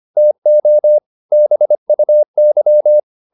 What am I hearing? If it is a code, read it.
Morse code, heard as a keyed tone, about 25 wpm, 600 Hz. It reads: TO BUY